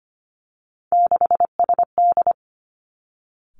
Morse code, heard as a keyed tone, about 25 wpm, 700 Hz. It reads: 6SD
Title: Three Challenging Letter-Number Combinations 25wpm